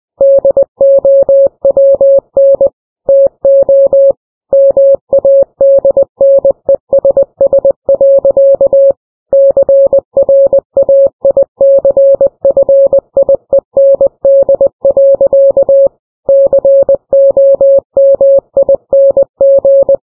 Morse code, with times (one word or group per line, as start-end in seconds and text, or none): 0.2-2.7 DOWN
3.1-4.1 TO
4.5-8.9 MADNESS.
9.3-15.9 CRAICFIEND.
16.3-20.0 COMING